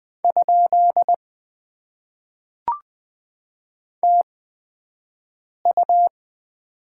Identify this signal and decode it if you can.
Morse code, heard as a keyed tone, about 20 wpm, spaced as for 5 wpm, 700 Hz. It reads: ? TU
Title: Top 90 QSO Elements - 8x Character Spacing - 20wpm